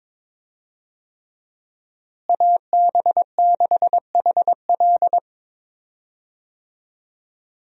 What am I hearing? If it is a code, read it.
Morse code, heard as a keyed tone, about 22 wpm, 700 Hz. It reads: AB6HL
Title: USA Call Signs - 5 Characters - Repeated 3x - 22wpm